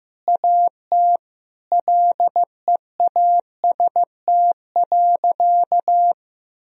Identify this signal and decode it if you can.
Morse code, heard as a keyed tone, about 15 wpm, 700 Hz. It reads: AT LEAST.
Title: Sentences from Top 1000 Words - Review Part 2 - 15wpm